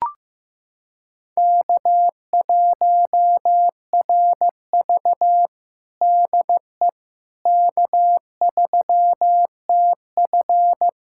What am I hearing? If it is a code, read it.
Morse code, heard as a keyed tone, about 15 wpm, 700 Hz. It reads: K1RV DE K3TF
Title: Practice QSO - Exchange 4 - Ending - 15wpm